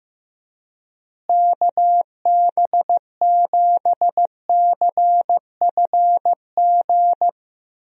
Morse code, heard as a keyed tone, about 15 wpm, 700 Hz. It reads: KB7CFG
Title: USA Call Signs - 6 Characters - Repeated 3x - 15wpm